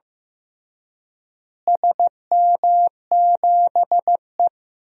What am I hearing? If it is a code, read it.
Morse code, heard as a keyed tone, about 15 wpm, 700 Hz. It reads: SM7E
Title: DXCC-100 Call Signs - 4 Characters - Repeated 3x - 15wpm